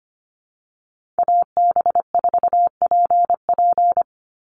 Morse code, heard as a keyed tone, about 25 wpm, 700 Hz. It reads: AB4PP
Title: NA SOTA Chaser Callsigns 25wpm